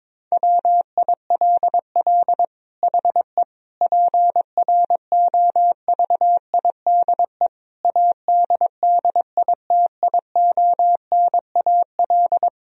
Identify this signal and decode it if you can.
Morse code, heard as a keyed tone, about 21 wpm, 700 Hz. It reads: WILL HE PROVIDE ADDITIONAL
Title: Sentences from Top 1000 Words - Review Part 4 - 22wpm